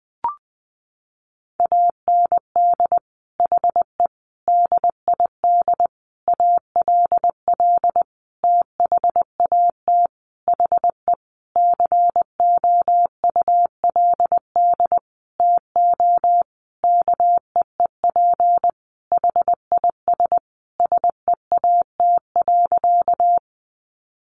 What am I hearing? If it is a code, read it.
Morse code, heard as a keyed tone, about 20 wpm, 700 Hz. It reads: AND HE DID ALL THAT HE COULD TO KEEP HIS SEAT.
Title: Sentences from Top 1500 Words 20wpm